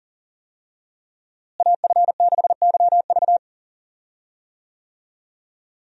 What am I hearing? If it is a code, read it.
Morse code, heard as a keyed tone, about 40 wpm, 700 Hz. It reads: AF6YV
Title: USA Call Signs - 5 Characters - Repeated 2x - 40wpm